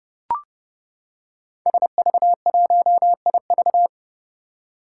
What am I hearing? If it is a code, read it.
Morse code, heard as a keyed tone, about 30 wpm, 700 Hz. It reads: SV1IV
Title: Five Challenging Letter-Number Combinations 30wpm